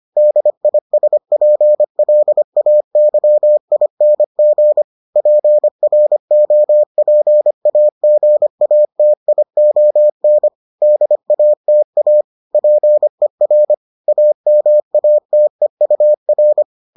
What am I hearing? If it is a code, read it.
Morse code, heard as a keyed tone, about 25 wpm, 600 Hz. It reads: DISPLAYING PROPAGATION DATA PER AMATEUR